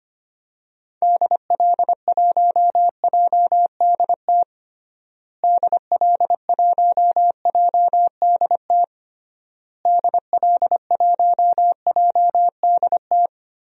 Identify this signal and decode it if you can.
Morse code, heard as a keyed tone, about 25 wpm, 700 Hz. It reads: DL1JDT DL1JDT DL1JDT
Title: DXCC-100 Call Signs - 6 Characters - Repeated 3x - 25wpm